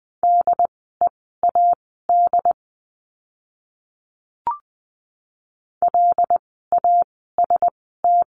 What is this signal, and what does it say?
Morse code, heard as a keyed tone, about 20 wpm, spaced as for 14 wpm, 700 Hz. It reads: DEAD LAST
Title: 075 Lesson 13f - Letter L - Words 20wpm